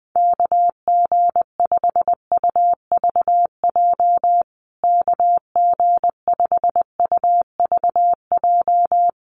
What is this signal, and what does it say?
Morse code, heard as a keyed tone, about 20 wpm, 700 Hz. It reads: KG5UVJ KG5UVJ
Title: USA Call Signs Repeated 3x 20wpm